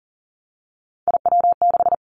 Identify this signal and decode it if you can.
Morse code, heard as a keyed tone, about 40 wpm, 700 Hz. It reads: IW6
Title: Three Challenging Letter-Number Combinations 40wpm